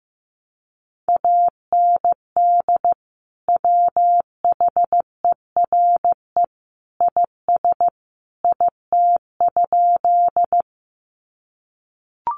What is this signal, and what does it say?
Morse code, heard as a keyed tone, about 15 wpm, 700 Hz. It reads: AND WHERE IS IT?
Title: Sentences from Top 200 Words 15wpm